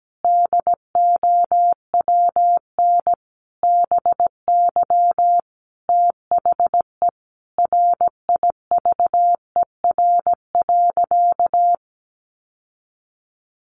Morse code, heard as a keyed tone, about 17 wpm, 700 Hz. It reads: DOWN BY THE RIVER.
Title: Sentences from Top 1200 Words 17wpm